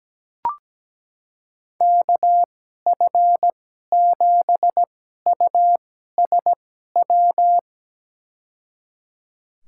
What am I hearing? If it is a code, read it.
Morse code, heard as a keyed tone, about 17 wpm, spaced as for 12 wpm, 700 Hz. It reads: KF7USW
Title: USA Call Signs - 6 Characters - 2x Character Spacing - 17wpm